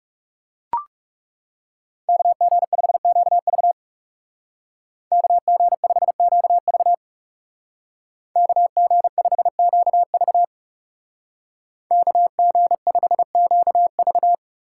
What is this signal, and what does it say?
Morse code, heard as a keyed tone, about 36 wpm, 700 Hz. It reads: KG5QV KG5QV KG5QV KG5QV
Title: Speed Racing - USA 2x2 Call Signs 30wpm